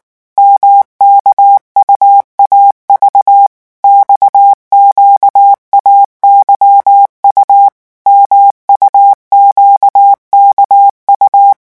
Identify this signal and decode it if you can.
Morse code, heard as a keyed tone, about 19 wpm, 800 Hz. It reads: MKUAV XQAYU MUQKU
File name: morse mp3 / revision2-vitesse 18 mots